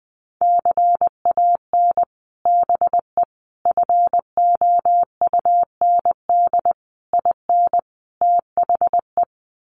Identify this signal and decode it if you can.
Morse code, heard as a keyed tone, about 20 wpm, 700 Hz. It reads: CAN BE FOUND IN THE
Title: Sets of 5 Words Encore 20wpm